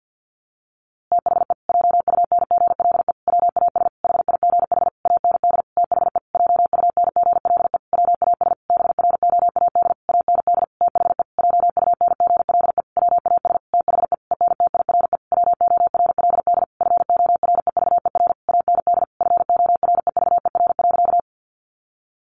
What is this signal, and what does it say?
Morse code, heard as a keyed tone, about 48 wpm, 700 Hz. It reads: THE JUNGLE WAS HIGH AND THE JUNGLE WAS BROAD AND THE JUNGLE WAS THE ENTIRE WORLD FOREVER AND FOREVER.
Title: A Sound Of Thunder 50wpm